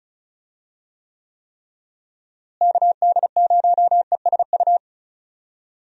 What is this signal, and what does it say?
Morse code, heard as a keyed tone, about 36 wpm, 700 Hz. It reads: KD0ESU
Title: USA Call Signs - 6 Characters - Repeated 3x - 35wpm